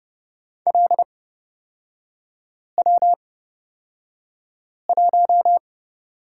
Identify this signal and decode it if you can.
Morse code, heard as a keyed tone, about 31 wpm, spaced as for 5 wpm, 690 Hz. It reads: LW1
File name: Lesson 38e - Letter q - Three Characters 30wpm